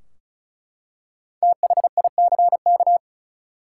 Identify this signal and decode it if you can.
Morse code, heard as a keyed tone, about 36 wpm, 700 Hz. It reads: THICK
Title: Five Letter Words - 35wpm